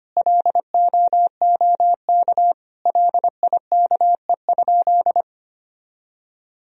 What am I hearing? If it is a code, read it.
Morse code, heard as a keyed tone, about 25 wpm, 700 Hz. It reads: LOOK LIKE?
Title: Sentences from Top 100 Words and Abbreviations 25wpm